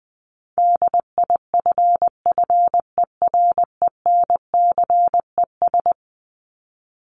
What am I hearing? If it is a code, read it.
Morse code, heard as a keyed tone, about 20 wpm, 700 Hz. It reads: DIFFERENCES
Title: ES:S Suffix - Top 100 Words with Base Words 20wpm